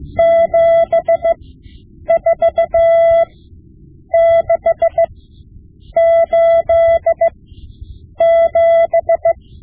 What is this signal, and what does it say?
Morse code, heard as a keyed tone, about 16 wpm, spaced as for 8 wpm, 670 Hz. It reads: 74687
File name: Morse Lessons / Morse Lessons / Tape-5B